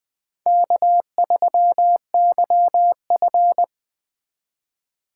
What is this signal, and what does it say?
Morse code, heard as a keyed tone, about 20 wpm, 700 Hz. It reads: K3YF